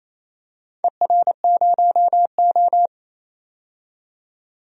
Morse code, heard as a keyed tone, about 28 wpm, 700 Hz. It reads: ER0O
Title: DXCC-100 Call Signs - 4 Characters - 28wpm